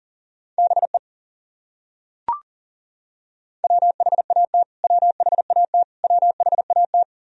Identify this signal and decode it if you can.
Morse code, heard as a keyed tone, about 40 wpm, 700 Hz. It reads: BE WHAT WHAT WHAT
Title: Top 100 Words Repeated 40wpm